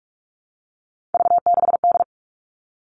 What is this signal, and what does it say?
Morse code, heard as a keyed tone, about 44 wpm, 700 Hz. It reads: V6D
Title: All Cut Numbers - 3-Digit 45wpm